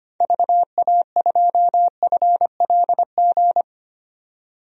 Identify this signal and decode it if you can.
Morse code, heard as a keyed tone, about 25 wpm, 700 Hz. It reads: VA2FLG
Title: DXCC-100 Call Signs - 6 Characters - 25wpm